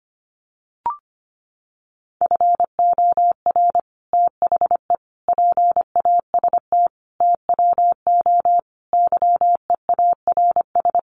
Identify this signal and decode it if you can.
Morse code, heard as a keyed tone, about 25 wpm, 700 Hz. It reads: FOR THE PAST TWO YEARS
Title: Sets of 5 Words Encore2 25wpm